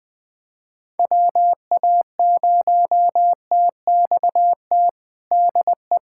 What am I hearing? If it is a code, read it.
Morse code, heard as a keyed tone, about 20 wpm, 700 Hz. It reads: WA0TXT DE
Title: USA Callsign DE Callsign 20wpm